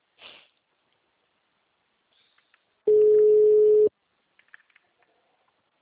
Ringback tone (Europe, 425 Hz).
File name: out- -2001- - - .7390